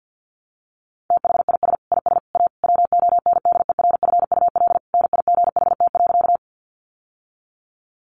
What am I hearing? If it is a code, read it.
Morse code, heard as a keyed tone, about 50 wpm, 700 Hz. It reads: THIS IS A WONDERFUL NIGHT.
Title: Sentences from Top 1100 Words 50wpm